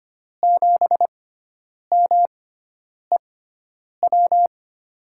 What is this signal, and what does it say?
Morse code, heard as a keyed tone, about 25 wpm, spaced as for 9 wpm, 700 Hz. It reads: 7MEW
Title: USA Call Signs - 5 Characters - 6x Character Spacing - 25wpm